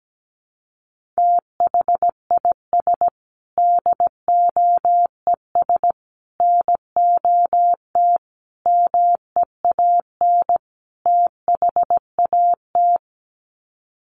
Morse code, heard as a keyed tone, about 17 wpm, 700 Hz. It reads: THIS DOES NOT MEAN THAT
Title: Sets of 5 Words Encore2 17wpm